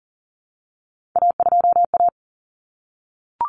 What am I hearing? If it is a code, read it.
Morse code, heard as a keyed tone, about 38 wpm, 700 Hz. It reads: A2A
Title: Common Cut Numbers - 3-Digit 40wpm